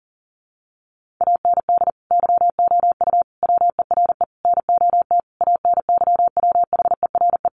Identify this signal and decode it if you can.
Morse code, heard as a keyed tone, about 40 wpm, 700 Hz. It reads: AND YOU WERE NOT ANYWHERE
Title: Sentences from Top 1400 Words 40wpm